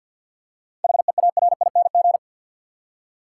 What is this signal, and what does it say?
Morse code, heard as a keyed tone, about 47 wpm, 700 Hz. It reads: HEARING
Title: Seven Letter Words - 50wpm